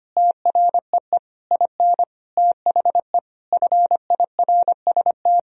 Morse code, heard as a keyed tone, about 25 wpm, 700 Hz. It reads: TREE IN THE FIRST
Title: Sentences from Top 1000 Words - Review Part 1 - 25wpm